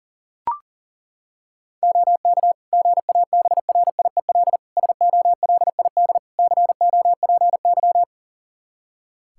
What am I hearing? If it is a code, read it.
Morse code, heard as a keyed tone, about 40 wpm, 690 Hz. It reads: OK GABRIEL SOLID COPY
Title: Practice QSO - Exchange 3 - More Info - 40wpm